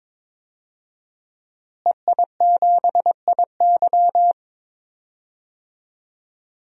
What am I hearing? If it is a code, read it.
Morse code, heard as a keyed tone, about 21 wpm, 700 Hz. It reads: EI7IY
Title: DXCC-100 Call Signs - 5 Characters - Repeated 3x - 22wpm